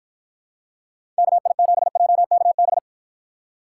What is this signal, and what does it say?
Morse code, heard as a keyed tone, about 53 wpm, 700 Hz. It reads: KI7JKB